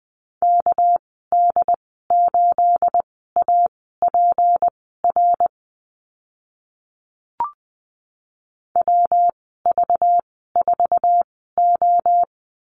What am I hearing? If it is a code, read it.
Morse code, heard as a keyed tone, about 20 wpm, spaced as for 15 wpm, 700 Hz. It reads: KD8APR WV4O
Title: 219 Lesson 35g - Number 0 - Call Signs 20wpm